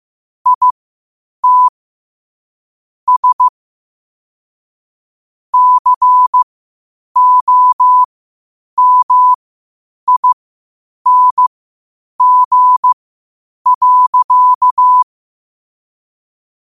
Morse code, heard as a keyed tone, about 15 wpm, spaced as for 8 wpm, 1000 Hz. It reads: ITS COMING.
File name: qotdfiles / 8wpm